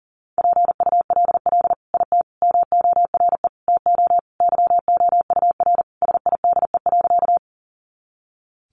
Morse code, heard as a keyed tone, about 40 wpm, 700 Hz. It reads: PULL IT MORE TO YOUR SIDE.
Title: Sentences from Top 400 Words 40wpm